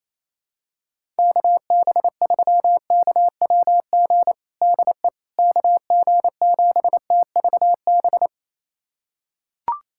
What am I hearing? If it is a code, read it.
Morse code, heard as a keyed tone, about 28 wpm, 700 Hz. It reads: KB3KWG DE KG7TVB
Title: USA Callsign DE Callsign 28wpm